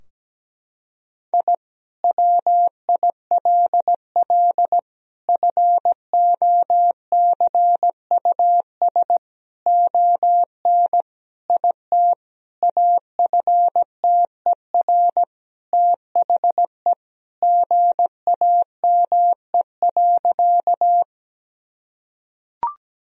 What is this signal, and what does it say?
Morse code, heard as a keyed tone, about 17 wpm, 700 Hz. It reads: I WILL FOCUS ON IT AFTER THE GAME.
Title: Sentences from Top 1000 Words - Review Part 4 - 17wpm